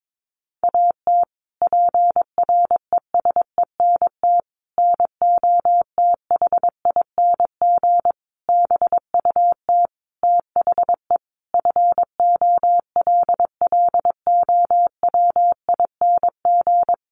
Morse code, heard as a keyed tone, about 22 wpm, 700 Hz. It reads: AT PRESENT NOTHING BUT THE FOLLOWING
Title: Sentences from Top 1500 Words 22wpm